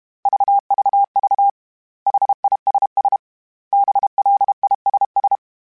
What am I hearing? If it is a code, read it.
Morse code, heard as a keyed tone, about 32 wpm, 800 Hz. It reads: VVV HISS BLISS
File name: General / General-mp3s / Suffixes @30